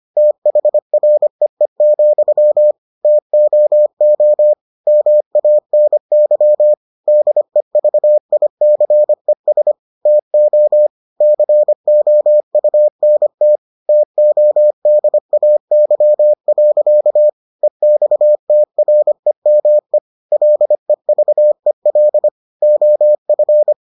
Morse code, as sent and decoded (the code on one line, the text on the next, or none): - .... .-. . --..-- / - --- --- / -- .- -. -.-- / -.. ...- .. -.-. ... / - --- / -.-. --- ..- -. - / - --- -.. .- -.-- .-.-.- / . -..- - .-. -- . / .-.. ...- . .-.. / --- ..-.
THREE, TOO MANY DEVICES TO COUNT TODAY. EXTREME LEVEL OF